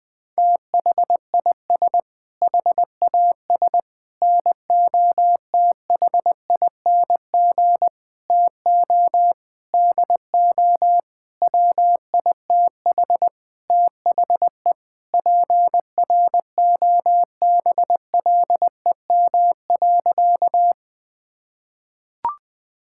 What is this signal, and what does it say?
Morse code, heard as a keyed tone, about 20 wpm, 700 Hz. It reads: THIS HAS NOTHING TO DO WITH THE PROBLEM.
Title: Sentences with 5-Word Phrases - Spoken Beginning - 20wpm